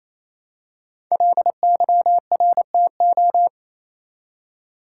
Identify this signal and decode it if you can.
Morse code, heard as a keyed tone, about 28 wpm, 700 Hz. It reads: LYRTO
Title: Code Groups - 5 Letters - 28wpm